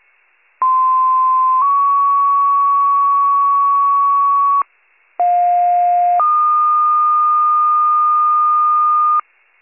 Two-tone page, heard as a one-second tone, then a three-second tone, about 1040 then 1120 Hz, about 710 then 1220 Hz. It is heard from 0.6 to 9.2 s.